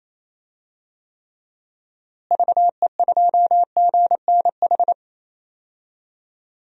Morse code, heard as a keyed tone, about 27 wpm, 700 Hz. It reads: VE2GNH
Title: DXCC-100 Call Signs - 6 Characters - Repeated 3x - 28wpm